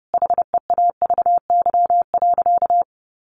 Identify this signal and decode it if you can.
Morse code, heard as a keyed tone, about 30 wpm, 700 Hz. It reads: HEAVY.
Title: Sentences from Top 1100 Words 30wpm